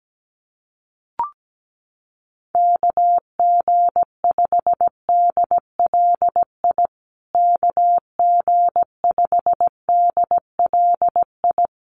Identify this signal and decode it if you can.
Morse code, heard as a keyed tone, about 17 wpm, 700 Hz. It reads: KG5DLI KG5DLI
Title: USA Call Signs Repeated 3x 17wpm